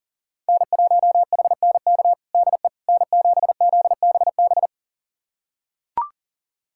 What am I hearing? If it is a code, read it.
Morse code, heard as a keyed tone, about 40 wpm, 700 Hz. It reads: N1HNK DE N7ZBB
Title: USA Callsign DE Callsign - 5 Characters - 40wpm